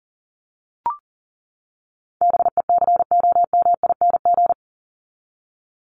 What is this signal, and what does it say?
Morse code, heard as a keyed tone, about 40 wpm, 700 Hz. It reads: BECOMING
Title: ING Suffix - Top 100 Words with Base Words 40wpm